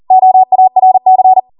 Morse code, heard as a keyed tone, about 40 wpm, 750 Hz. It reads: OARC